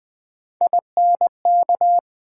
Morse code, heard as a keyed tone, about 20 wpm, 700 Hz. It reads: INK
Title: Build A Word - 1 Letter At A Time - 3-Letter Words - 20wpm